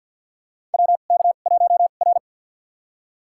Morse code, heard as a keyed tone, about 47 wpm, spaced as for 35 wpm, 700 Hz. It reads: WK1R
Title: USA Call Signs - 4 Characters - 2x Character Spacing - 50wpm